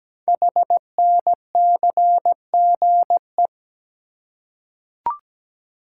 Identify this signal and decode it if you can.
Morse code, heard as a keyed tone, about 17 wpm, 700 Hz. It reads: HNCGE
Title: Code Groups - 5 Letters - 17wpm